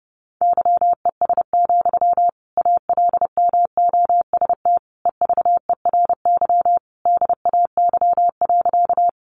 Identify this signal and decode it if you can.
Morse code, heard as a keyed tone, about 30 wpm, 700 Hz. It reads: YES, ALMOST EVERY DAY.
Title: Sentences from Top 700 Words 30wpm